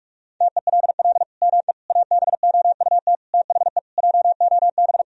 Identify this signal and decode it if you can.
Morse code, heard as a keyed tone, about 46 wpm, 700 Hz. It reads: TELL ME ABOUT THE JOB
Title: Sentences from Top 1000 Words - Review Part 3 - 45wpm